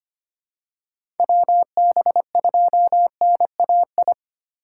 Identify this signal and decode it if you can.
Morse code, heard as a keyed tone, about 25 wpm, 700 Hz. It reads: WB2NAI
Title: USA Call Signs - 6 Characters - 4x Character Spacing - 25wpm